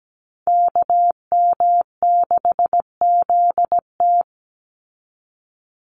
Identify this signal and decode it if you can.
Morse code, heard as a keyed tone, about 17 wpm, 700 Hz. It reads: KM6ZT
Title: Call Signs Repeated 2x - 17wpm